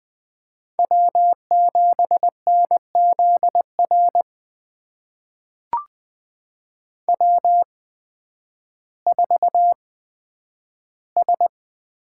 Morse code, heard as a keyed tone, about 20 wpm, spaced as for 11 wpm, 700 Hz. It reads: W7NZR W4S
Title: USA Call Signs - 5 Characters - 8x Character Spacing - 20wpm